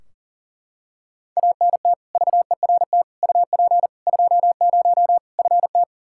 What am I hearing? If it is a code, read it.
Morse code, heard as a keyed tone, about 40 wpm, 700 Hz. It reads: ANT VERT UP 20 FT
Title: Practice QSO - Exchange 2 - Info - 40wpm